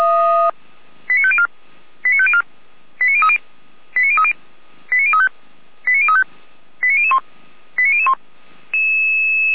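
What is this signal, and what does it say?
Five-tone selective call (ZVEI), heard as quick runs of five tones, tones of about 70 ms. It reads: 89504 89504 89030 89030 89035 89035 89001 89001
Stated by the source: ZVEI five-tone selective call 89099 received on SDR, from fire brigade Feuerwehr Orlamünde.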